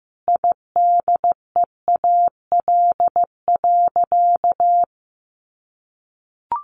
Morse code, heard as a keyed tone, about 14 wpm, 700 Hz. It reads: IDEAL.
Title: Sentences from Top 1400 Words 15wpm